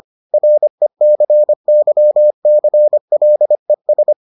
Morse code, heard as a keyed tone, about 25 wpm, 600 Hz. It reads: RECYCLES